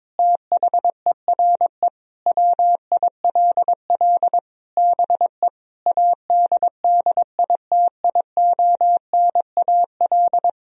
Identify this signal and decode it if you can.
Morse code, heard as a keyed tone, about 22 wpm, 700 Hz. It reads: THERE WILL BE ADDITIONAL
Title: Sentences from Top 1000 Words - Review Part 4 - 22wpm